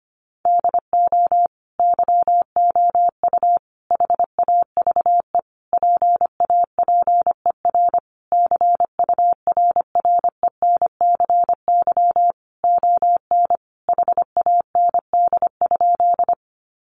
Morse code, heard as a keyed tone, about 25 wpm, 700 Hz. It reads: DO YOU HAVE PAPER CURRENCY ON HAND?
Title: Sentences from Top 1500 Words 25wpm